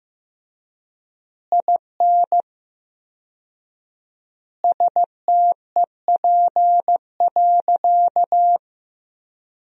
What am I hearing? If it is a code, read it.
Morse code, heard as a keyed tone, about 15 wpm, 700 Hz. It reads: IN STEP.
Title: 4x Word-Spacing - Sentences from Top 400 Words - 15wpm